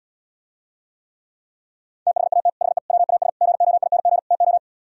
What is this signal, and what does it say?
Morse code, heard as a keyed tone, about 77 wpm, 700 Hz. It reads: THAT HE WAS GOING TO